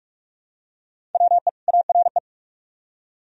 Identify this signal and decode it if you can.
Morse code, heard as a keyed tone, about 44 wpm, 700 Hz. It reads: WE ARE